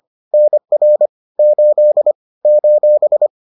Morse code, heard as a keyed tone, about 25 wpm, 600 Hz. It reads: NR 8 :
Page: bulletin UFT sous forme texte et en CW